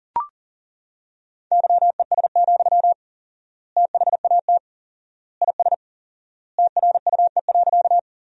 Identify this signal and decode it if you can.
Morse code, heard as a keyed tone, about 40 wpm, 700 Hz. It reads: YES, THAT IS TRUE.
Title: 4x Word-Spacing - Sentences from Top 500 Words - 40wpm